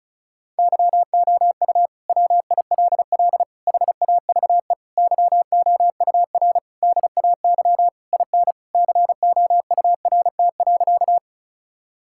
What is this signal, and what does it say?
Morse code, heard as a keyed tone, about 36 wpm, 700 Hz. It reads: YOU WILL HAVE YOUR DAY IN COURT.
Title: Sentences from Top 1600 Words 35wpm